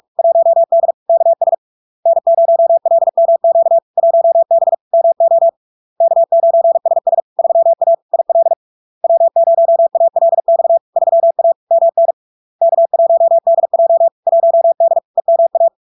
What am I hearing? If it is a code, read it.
Morse code, heard as a keyed tone, about 46 wpm, 670 Hz. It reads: 1D KS N0LMQ 1B MO K9SS 3A IL W0ALX 3A MN K1DJ 1D EMA
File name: LARK-FD-2022-Log-in-CW-45wpm-670-Hz